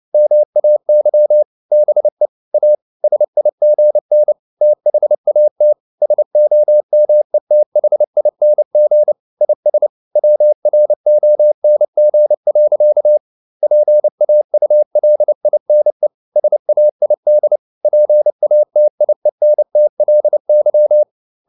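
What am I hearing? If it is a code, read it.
Morse code, heard as a keyed tone, about 29 wpm, 600 Hz. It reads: MAY BE A SIGN THAT SOMETHING IS WRONG. PAULINE SAID PATIENTLY
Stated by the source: Morse code at 29 wpm, sending Ken Follett's novel Never in English, from CW med Gnister.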